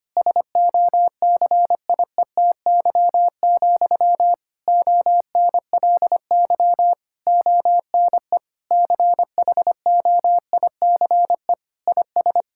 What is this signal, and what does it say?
Morse code, heard as a keyed tone, about 25 wpm, 700 Hz. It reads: SOCIETY, ONLY ONE CHOICE IS
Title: Sentences from Top 1000 Words - Review Part 6 - 25wpm